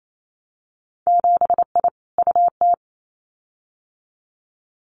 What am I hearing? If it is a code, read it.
Morse code, heard as a keyed tone, about 28 wpm, 700 Hz. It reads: 7I UT
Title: Winter Field Day Exchanges Encore 28wpm